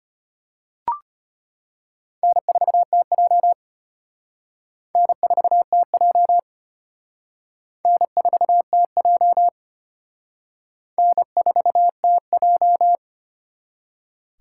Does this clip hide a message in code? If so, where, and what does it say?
Yes: Morse code from 2.2 to 13.0 s, heard as a keyed tone, about 31 wpm, 700 Hz. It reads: N4TJ N4TJ N4TJ N4TJ